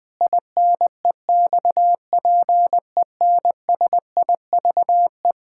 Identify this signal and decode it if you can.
Morse code, heard as a keyed tone, about 21 wpm, 700 Hz. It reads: INEXPENSIVE
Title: Pangram Challenge v2 - 20wpm